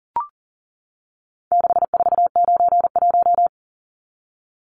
Morse code, heard as a keyed tone, about 40 wpm, 700 Hz. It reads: 6491
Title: Numbers - 4-Digit 40wpm